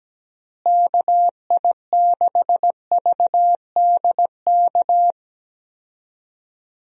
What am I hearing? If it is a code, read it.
Morse code, heard as a keyed tone, about 17 wpm, 700 Hz. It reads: KI6VDK
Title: USA Call Signs - 6 Characters - Repeated 2x - 17wpm